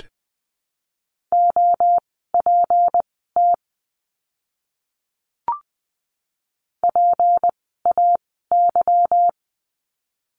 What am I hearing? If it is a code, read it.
Morse code, heard as a keyed tone, about 20 wpm, spaced as for 15 wpm, 700 Hz. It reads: OPT PAY
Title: 153 Lesson 25f - Letter p - Words 20wpm